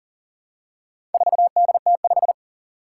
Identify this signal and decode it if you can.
Morse code, heard as a keyed tone, about 38 wpm, 700 Hz. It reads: 4DT5
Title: Four Challenging Letter-Number Combinations 40wpm